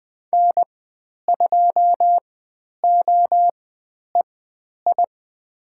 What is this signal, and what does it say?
Morse code, heard as a keyed tone, about 20 wpm, spaced as for 10 wpm, 700 Hz. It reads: N2OEI
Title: 095 Lesson 16g - Letter c - Call Signs 20wpm